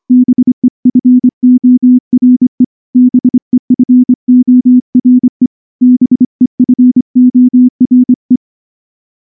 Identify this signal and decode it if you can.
Morse code, heard as a keyed tone, about 25 wpm, 250 Hz. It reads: BEFORE BEFORE BEFORE
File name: morse / morse-training-25WPM